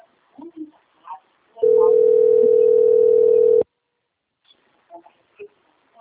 Ringback tone (North America), 440 + 480 Hz.
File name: out- - - - - .29656